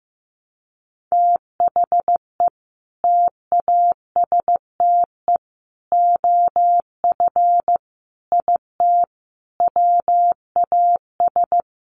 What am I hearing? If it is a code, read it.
Morse code, heard as a keyed tone, about 15 wpm, 700 Hz. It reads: THE TASTE OF IT WAS
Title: Sentences from Top 1100 Words 15wpm